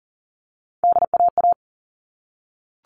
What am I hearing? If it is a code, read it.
Morse code, heard as a keyed tone, about 40 wpm, 700 Hz. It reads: DAA
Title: All Cut Numbers - 3-Digit 40wpm